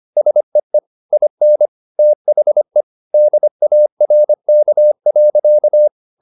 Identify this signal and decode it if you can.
Morse code, heard as a keyed tone, about 25 wpm, 600 Hz. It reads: SEE IN THE DARK.
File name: the old man and the sea part 1 25wpm